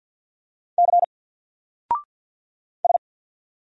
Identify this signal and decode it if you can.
Morse code, heard as a keyed tone, about 50 wpm, 700 Hz. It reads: C S
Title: Single Letter-Number 50wpm